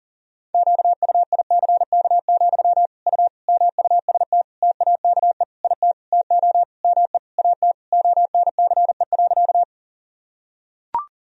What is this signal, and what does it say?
Morse code, heard as a keyed tone, about 40 wpm, 700 Hz. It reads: QUICK, U MUST TAKE IT TO ME AT ONCE.
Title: Sentences from Top 500 Words and Abbreviations 40wpm